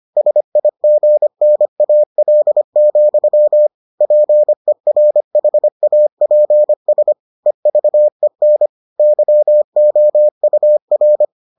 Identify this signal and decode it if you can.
Morse code, heard as a keyed tone, about 25 wpm, 600 Hz. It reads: SIGNAL, PERHAPS EVEN YOUR